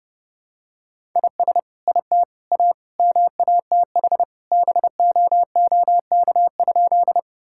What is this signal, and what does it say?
Morse code, heard as a keyed tone, about 30 wpm, 700 Hz. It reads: IS IT A MATH BOOK?
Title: Sentences from Top 1000 Words - Review Part 6 - 30wpm